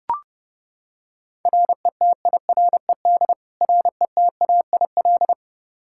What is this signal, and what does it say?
Morse code, heard as a keyed tone, about 30 wpm, 700 Hz. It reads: RETIRED RETAIL
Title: K1USN SST QSO Practice 30wpm